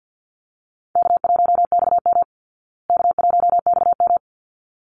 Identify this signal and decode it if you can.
Morse code, heard as a keyed tone, about 50 wpm, 700 Hz. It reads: K1XM K1XM
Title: Easy Call Signs Repeated 2x - 50wpm